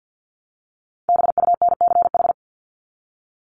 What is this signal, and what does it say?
Morse code, heard as a keyed tone, about 50 wpm, 700 Hz. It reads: BUNCH
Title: Top 1751-2000 Words 50wpm